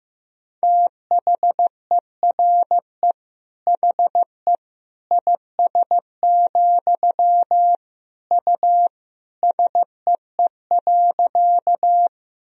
Morse code, heard as a keyed tone, about 15 wpm, 700 Hz. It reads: THERE HE IS, U SEE.
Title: Sentences from Top 100 Words and Abbreviations 15wpm